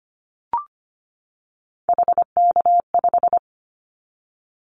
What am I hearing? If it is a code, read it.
Morse code, heard as a keyed tone, about 25 wpm, 700 Hz. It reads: HK5
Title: Callsign Prefixes for Top 100 DXCC - 25wpm